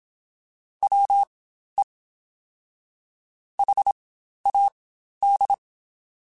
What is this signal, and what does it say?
Morse code, heard as a keyed tone, about 27 wpm, spaced as for 12 wpm, 800 Hz. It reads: WE HAD
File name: Spaced Characters / Spaced Characters-mp3s / Weather4 24 sp @25